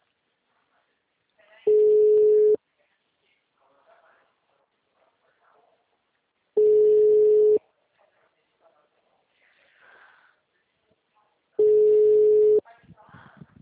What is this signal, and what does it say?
Ringback tone (Europe, 425 Hz), with speech.